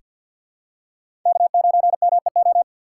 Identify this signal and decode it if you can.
Morse code, heard as a keyed tone, about 47 wpm, 700 Hz. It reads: K9MEO
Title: USA Call Signs - 4x Character Spacing - 50wpm